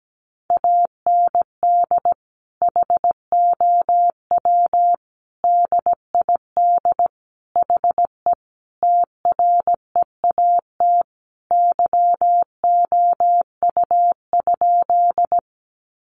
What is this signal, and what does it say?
Morse code, heard as a keyed tone, about 17 wpm, 700 Hz. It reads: AND HOW DID HE TREAT YOU?
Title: Sentences from Top 1300 Words 17wpm